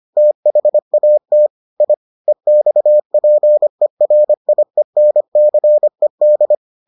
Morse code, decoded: THAT I EXPERIENCED

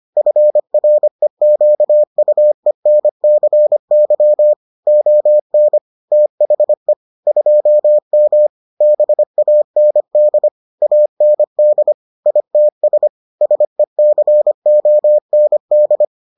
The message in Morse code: FREQUENCY ON THE 2M BAND AND ITS SECOND

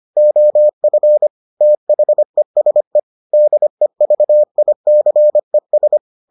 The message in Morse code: OF THESE DEVICES

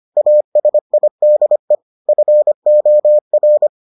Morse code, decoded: ASIDE FOR

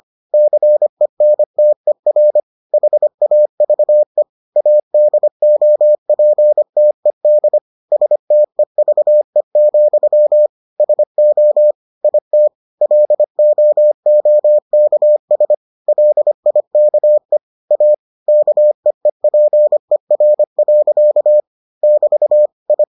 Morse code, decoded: CENTER HAVE ADOPTED STEVE, SO IT LOOKS LIKE A KEEPER. <BT> I

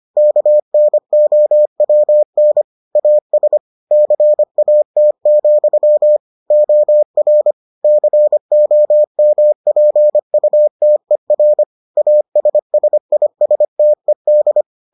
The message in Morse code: KNOWN AS CAT, OR COMPUTER ASSISTED

